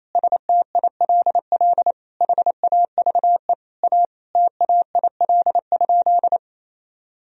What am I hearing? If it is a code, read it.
Morse code, heard as a keyed tone, about 27 wpm, 700 Hz. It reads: STILL HAVE A TAIL?
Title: Sentences from Top 1000 Words - Review Part 3 - 28wpm